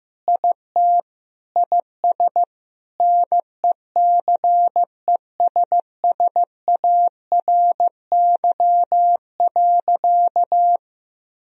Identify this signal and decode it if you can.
Morse code, heard as a keyed tone, about 15 wpm, 700 Hz. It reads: IT IS NECESSARY.
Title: Sentences from Top 1000 Words - Review Part 4 - 15wpm